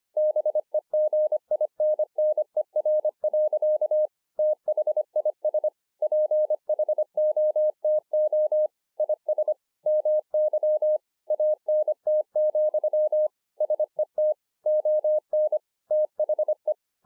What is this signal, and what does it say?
Morse code, heard as a keyed tone, about 26 wpm, 610 Hz. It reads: BEGINNER. THIS PHOTO IS MY ANT, SET ON THE
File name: 2011-04-mp3 / BEGINNERS DIARY 25WPM